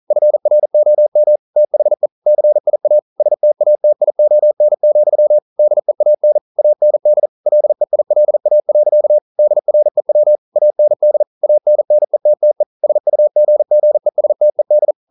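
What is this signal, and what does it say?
Morse code, heard as a keyed tone, about 42 wpm, 600 Hz. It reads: FROM THE CIA STATION, DEAN AND LEILA. DREW AND ANNETTE SUGGESTED
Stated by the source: Morse code practice at 41 wpm, from CW med Gnister.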